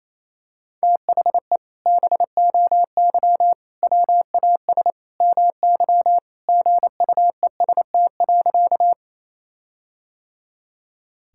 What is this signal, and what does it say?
Morse code, heard as a keyed tone, about 28 wpm, 700 Hz. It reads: THE BOY WAS MY GUEST.
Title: Sentences from Top 1600 Words 28wpm